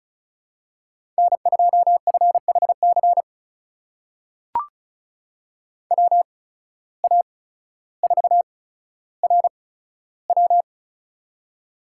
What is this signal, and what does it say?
Morse code, heard as a keyed tone, about 36 wpm, spaced as for 15 wpm, 700 Hz. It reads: N2FHC WA4RW
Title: USA Call Signs - 5 Characters - 8x Character Spacing - 35wpm